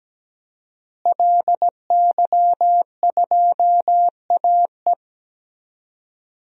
Morse code, heard as a keyed tone, about 17 wpm, 700 Hz. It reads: LY2AE
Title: DXCC-100 Call Signs - 5 Characters - Repeated 2x - 17wpm